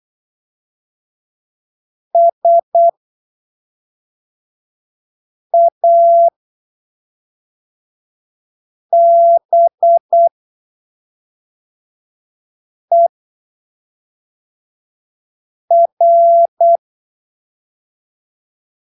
Morse code, heard as a keyed tone, about 8 wpm, spaced as for 3 wpm, 680 Hz. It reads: SABER
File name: arquivos / texto / texto 15 08ppm